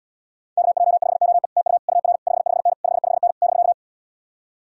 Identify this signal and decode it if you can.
Morse code, heard as a keyed tone, about 77 wpm, 700 Hz. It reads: KJ5OE TU UR 55N 55N <BK>